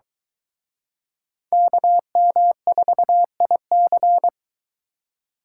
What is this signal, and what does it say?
Morse code, heard as a keyed tone, about 23 wpm, 700 Hz. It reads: KM4IC